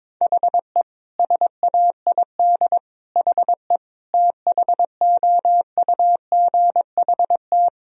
Morse code, heard as a keyed tone, about 22 wpm, 700 Hz. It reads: HE SAID HE THOUGHT
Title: Sentences from Top 1000 Words - Review Part 3 - 22wpm